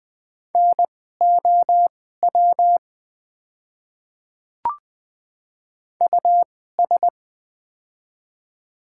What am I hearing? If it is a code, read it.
Morse code, heard as a keyed tone, about 20 wpm, spaced as for 14 wpm, 700 Hz. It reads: NOW US
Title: 2x Character Spacing - Top 200 Words - 20wpm